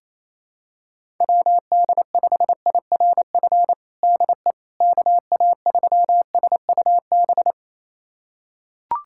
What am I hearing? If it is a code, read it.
Morse code, heard as a keyed tone, about 28 wpm, 700 Hz. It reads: WD5IRF DE KA3SUB
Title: USA Callsign DE Callsign - 6 Characters - 28wpm